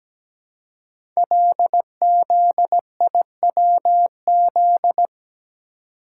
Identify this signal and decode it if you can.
Morse code, heard as a keyed tone, about 17 wpm, 700 Hz. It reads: LZIWZ
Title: Code Groups - 5 Letters - 17wpm